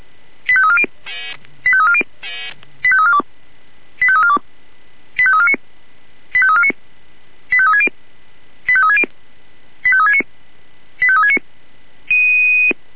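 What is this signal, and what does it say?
Five-tone selective call (ZVEI), heard as quick runs of five tones, tones of about 70 ms. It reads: 85340 85340 85342 85342 85349 85349 85360 85360 85369 85369
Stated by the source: ZVEI five-tone selective call, address 85259, received on an SDR.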